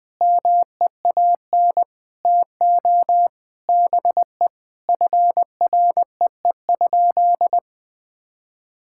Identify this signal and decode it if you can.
Morse code, heard as a keyed tone, about 21 wpm, 700 Hz. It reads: MEAN TO BE FREE?
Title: Sentences with 6-Word Phrases - 20wpm